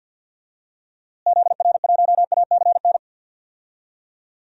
Morse code, heard as a keyed tone, about 48 wpm, 700 Hz. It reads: ZR1ACN
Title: DXCC-100 Call Signs - 6 Characters - Repeated 3x - 50wpm